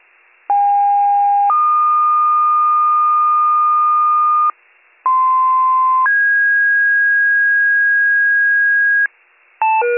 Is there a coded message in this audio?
Yes: two-tone page from 0.5 to 9.1 s, heard as a one-second tone, then a three-second tone, about 810 then 1220 Hz, about 1040 then 1650 Hz, with noise.